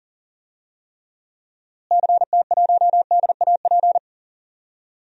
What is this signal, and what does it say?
Morse code, heard as a keyed tone, about 40 wpm, 700 Hz. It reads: CT1DAP
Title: DXCC-100 Call Signs - 6 Characters - Repeated 3x - 40wpm